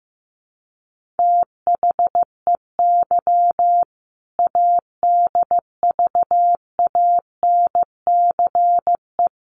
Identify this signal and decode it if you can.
Morse code, heard as a keyed tone, about 15 wpm, 700 Hz. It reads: THEY ADVANCE
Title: Sentences from Top 1400 Words 15wpm